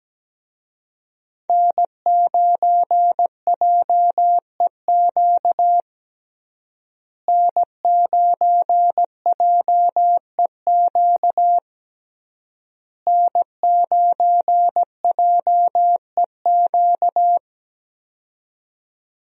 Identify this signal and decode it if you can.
Morse code, heard as a keyed tone, about 17 wpm, 700 Hz. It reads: N9JEQ N9JEQ N9JEQ
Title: USA Call Signs - 5 Characters - Repeated 3x - 17wpm